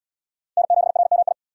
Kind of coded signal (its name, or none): Morse code